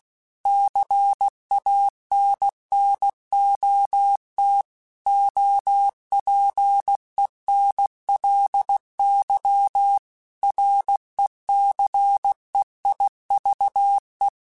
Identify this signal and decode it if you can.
Morse code, heard as a keyed tone, about 16 wpm, 800 Hz. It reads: CANNOT OPENLY RECEIVE
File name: Quotes / Quotes-mp3s / Advice @15